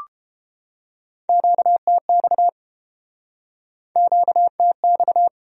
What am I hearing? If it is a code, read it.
Morse code, heard as a keyed tone, about 33 wpm, 700 Hz. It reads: QTX QTX